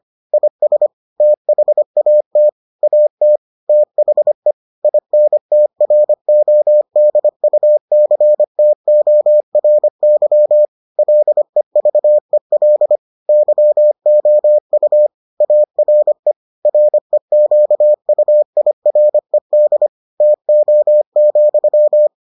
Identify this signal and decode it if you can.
Morse code, heard as a keyed tone, about 25 wpm, 600 Hz. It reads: IS THAT AT THE INTRODUCTORY LEVEL YOU ARE REQUIRED TO,